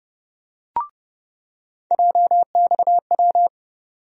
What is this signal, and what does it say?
Morse code, heard as a keyed tone, about 31 wpm, 700 Hz. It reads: JXW
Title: USA Call Sign Postfixes - 3 Letters 30wpm